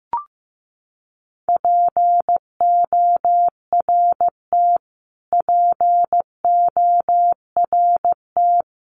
Morse code, heard as a keyed tone, about 15 wpm, 700 Hz. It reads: PORT PORT
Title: Top 200 Words Repeated 15wpm